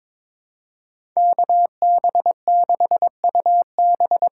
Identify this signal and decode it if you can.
Morse code, heard as a keyed tone, about 21 wpm, 700 Hz. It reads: KB6UB
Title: USA Call Signs - 6 Characters - 8x Character Spacing - 22wpm